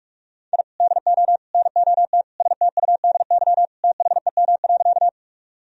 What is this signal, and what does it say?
Morse code, heard as a keyed tone, about 41 wpm, 700 Hz. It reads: I DO NOT STUDY THEM.
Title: Sentences from Top 1000 Words - Review Part 1 - 45wpm